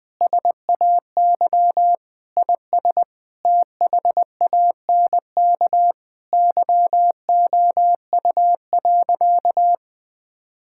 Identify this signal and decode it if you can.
Morse code, heard as a keyed tone, about 20 wpm, 700 Hz. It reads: SAY IS THANK YOU.
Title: Sentences with 8-Word Phrases - 20wpm